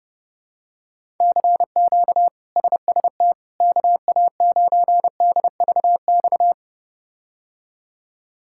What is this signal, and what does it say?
Morse code, heard as a keyed tone, about 30 wpm, 700 Hz. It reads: CQ SST KA9DVX
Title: K1USN SST QSO Practice - 30wpm